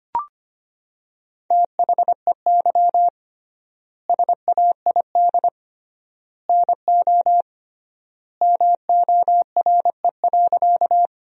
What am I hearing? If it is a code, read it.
Morse code, heard as a keyed tone, about 25 wpm, 700 Hz. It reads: THEY SAID NO MORE.
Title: 3x Word-Spacing - Sentences from Top 100 Words - 25wpm